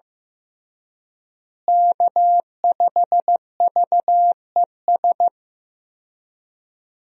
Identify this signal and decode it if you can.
Morse code, heard as a keyed tone, about 15 wpm, 700 Hz. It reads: K5VES
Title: USA Call Signs - 5 Characters - Repeated 3x - 15wpm